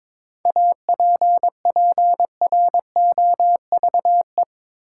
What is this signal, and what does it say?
Morse code, heard as a keyed tone, about 21 wpm, 700 Hz. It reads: APPROVE
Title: Seven Letter Words - 22wpm